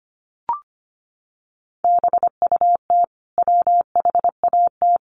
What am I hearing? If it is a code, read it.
Morse code, heard as a keyed tone, about 25 wpm, 700 Hz. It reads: BUT WHAT
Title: Sentences from Top 200 Words 25wpm